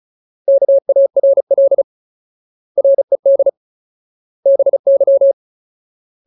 Morse code, heard as a keyed tone, about 34 wpm, 550 Hz. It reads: KARL RED BY
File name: Ultimate Word List / 35WPM Ultimate